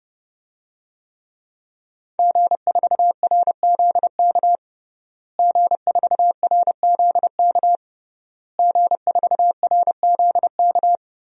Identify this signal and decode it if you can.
Morse code, heard as a keyed tone, about 30 wpm, 700 Hz. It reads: G4RZK G4RZK G4RZK
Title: DXCC-100 Call Signs - 5 Characters - Repeated 3x - 30wpm